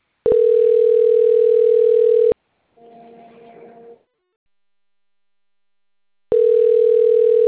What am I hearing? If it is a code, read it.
Ringback tone (North America), 440 + 480 Hz.